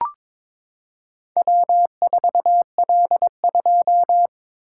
Morse code, heard as a keyed tone, about 22 wpm, 700 Hz. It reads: W4L2
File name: practice / 22wpm / Four Letter-Number Combinations - 22wpm